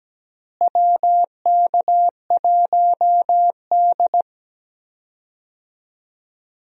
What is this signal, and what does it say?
Morse code, heard as a keyed tone, about 17 wpm, 700 Hz. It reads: WK1D
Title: USA Call Signs - 4 Characters - Repeated 2x - 17wpm